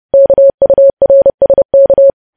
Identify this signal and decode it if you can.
Morse code, heard as a keyed tone, about 30 wpm, 550 Hz. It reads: KURSK
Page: news headlines in morse code at 30 wpm